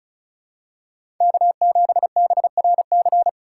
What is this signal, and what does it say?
Morse code, heard as a keyed tone, about 34 wpm, 700 Hz. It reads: K7BRC
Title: USA Call Signs - 5 Characters - 3x Character Spacing - 35wpm